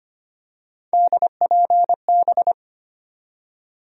Morse code, heard as a keyed tone, about 25 wpm, 700 Hz. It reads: DPB
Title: USA Call Sign Postfixes - 3 Letters 25wpm